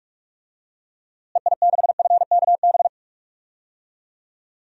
Morse code, heard as a keyed tone, about 41 wpm, 700 Hz. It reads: EI6FKB
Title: DXCC-100 Call Signs - 6 Characters - Repeated 2x - 45wpm